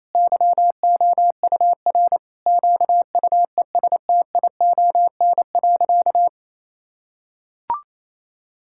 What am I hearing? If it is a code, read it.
Morse code, heard as a keyed tone, about 28 wpm, 700 Hz. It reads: YOUR QUESTION.
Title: Sentences from Top 1000 Words - Review Part 6 - 28wpm